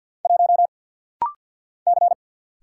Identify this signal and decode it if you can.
Morse code, heard as a keyed tone, about 48 wpm, 700 Hz. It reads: 1 C
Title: Single Letter-Number Alt - Rapid-Fire 50wpm